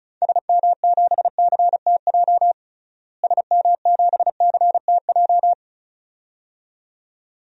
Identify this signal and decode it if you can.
Morse code, heard as a keyed tone, about 36 wpm, 700 Hz. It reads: SM7CTJ SM7CTJ
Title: DXCC-100 Call Signs - 6 Characters - Repeated 2x - 35wpm